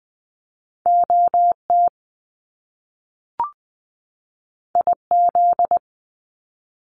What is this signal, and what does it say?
Morse code, heard as a keyed tone, about 20 wpm, 700 Hz. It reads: OT IZ
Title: Top Two Letter Combinations in Words 20wpm